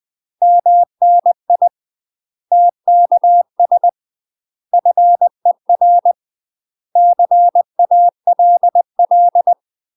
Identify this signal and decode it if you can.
Morse code, heard as a keyed tone, about 20 wpm, 700 Hz. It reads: MNI TKS FER CALL
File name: UK MASTER QSOs TRAINING / 20WPM / UK MASTER QSOs 0043